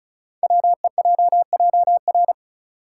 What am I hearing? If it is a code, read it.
Morse code, heard as a keyed tone, about 36 wpm, 700 Hz. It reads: WEJJR
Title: Code Groups - 5 Letters - 35wpm